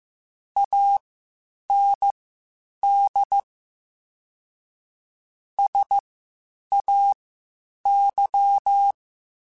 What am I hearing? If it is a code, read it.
Morse code, heard as a keyed tone, about 15 wpm, spaced as for 8 wpm, 800 Hz. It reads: AND SAY